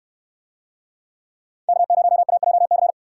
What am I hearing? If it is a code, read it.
Morse code, heard as a keyed tone, about 64 wpm, 700 Hz. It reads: K0AJC